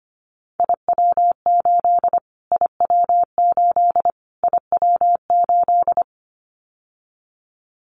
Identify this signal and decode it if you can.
Morse code, heard as a keyed tone, about 25 wpm, 700 Hz. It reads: IW8 IW8 IW8
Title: Callsign Prefixes for Top 25 DXCC - Repeated 3x - 25wpm